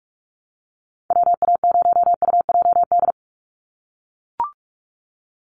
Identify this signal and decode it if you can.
Morse code, heard as a keyed tone, about 44 wpm, 700 Hz. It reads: WA0UJD
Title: USA Call Signs 45wpm